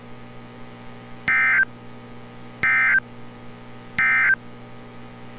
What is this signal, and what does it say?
Emergency Alert System (SAME) tones, heard as harsh buzzing bursts. It is heard from 1.2 to 4.4 s.
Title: NOAA Weather Radio - Altoona, PA